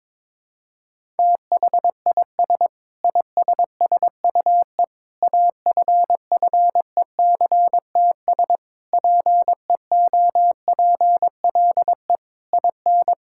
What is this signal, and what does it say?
Morse code, heard as a keyed tone, about 22 wpm, 700 Hz. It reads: THIS ISSUE AFFECTS PEOPLE IN
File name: Sentences with 7-Word Phrases - 22wpm